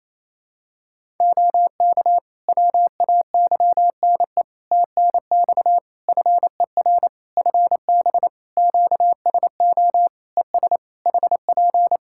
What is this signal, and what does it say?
Morse code, heard as a keyed tone, about 28 wpm, 700 Hz. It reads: OK WAYNE TNX FER FB QSO ES HP
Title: Practice QSO - Exchange 4 - Ending - 28wpm